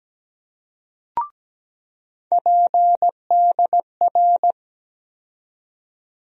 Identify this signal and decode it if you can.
Morse code, heard as a keyed tone, about 17 wpm, 700 Hz. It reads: PDR
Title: USA Call Sign Postfixes - 3 Letters 17wpm